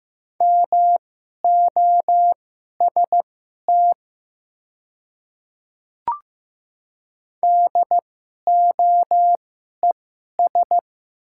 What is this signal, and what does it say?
Morse code, heard as a keyed tone, about 15 wpm, spaced as for 11 wpm, 700 Hz. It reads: MOST DOES
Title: 2x Character Spacing - Top 200 Words - 15wpm